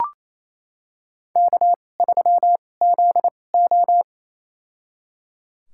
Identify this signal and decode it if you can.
Morse code, heard as a keyed tone, about 28 wpm, spaced as for 20 wpm, 700 Hz. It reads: K3ZO
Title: USA Call Signs - 4 Characters - 2x Character Spacing - 28wpm